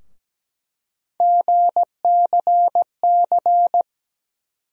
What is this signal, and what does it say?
Morse code, heard as a keyed tone, about 17 wpm, 700 Hz. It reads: GCC
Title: USA Call Sign Postfixes - 3 Letters 17wpm